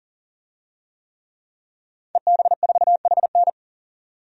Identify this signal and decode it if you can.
Morse code, heard as a keyed tone, about 40 wpm, 700 Hz. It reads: EB4HN
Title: DXCC-100 Call Signs - 5 Characters - 40wpm